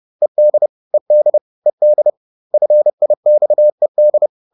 Morse code, heard as a keyed tone, about 30 wpm, 600 Hz. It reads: ED ED ED FIXED